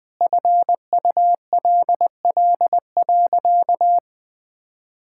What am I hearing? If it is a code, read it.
Morse code, heard as a keyed tone, about 20 wpm, 700 Hz. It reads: FULL.
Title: Sentences from Top 1600 Words 20wpm